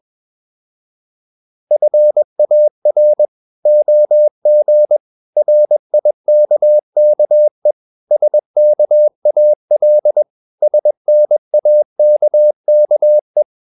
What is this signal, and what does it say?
Morse code, heard as a keyed tone, about 21 wpm, 600 Hz. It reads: FAR OG RIKKE SKAL SNAKKE
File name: Ferie QRS 21wpm